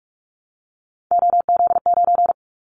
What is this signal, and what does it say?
Morse code, heard as a keyed tone, about 46 wpm, 700 Hz. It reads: OZ9
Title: Callsign Prefixes for Top 100 DXCC - 45wpm